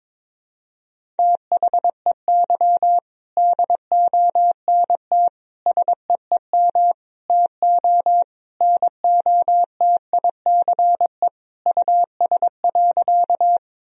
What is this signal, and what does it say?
Morse code, heard as a keyed tone, about 22 wpm, 700 Hz. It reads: THEY DONT SEEM TO NOTICE US.
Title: Sentences from Top 1000 Words - Review Part 2 - 22wpm